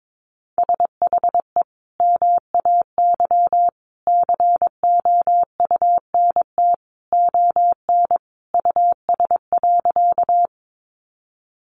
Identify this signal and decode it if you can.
Morse code, heard as a keyed tone, about 22 wpm, 700 Hz. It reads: SHE MAY COUNT ON US.
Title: Sentences from Top 1500 Words 22wpm